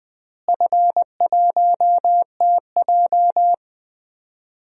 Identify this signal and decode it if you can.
Morse code, heard as a keyed tone, about 20 wpm, 700 Hz. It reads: F1TJ
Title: DXCC-100 Call Signs - 4 Characters - Repeated 2x - 20wpm